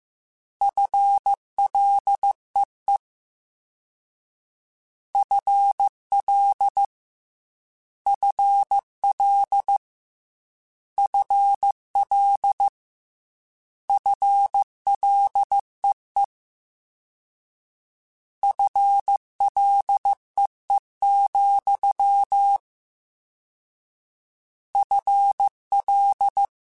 Morse code, heard as a keyed tone, about 15 wpm, 800 Hz. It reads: FLEE FL FL FL FLEE FLEE, FL
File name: Combinations / Combinations-mp3s / Combinations@13